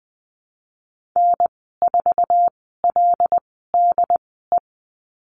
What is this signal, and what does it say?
Morse code, heard as a keyed tone, about 20 wpm, spaced as for 14 wpm, 700 Hz. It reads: N4LDE
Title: 147 Lesson 24g - Letter y - Call Signs 20wpm